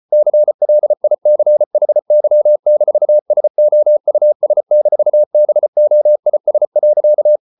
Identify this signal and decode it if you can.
Morse code, heard as a keyed tone, about 34 wpm, 600 Hz. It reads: CLICHY-SOUS-BOIS.